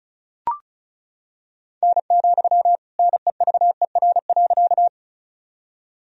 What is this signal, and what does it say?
Morse code, heard as a keyed tone, about 36 wpm, 700 Hz. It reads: N, NEVER.
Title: Sentences from Top 300 Words and Abbreviations 35wpm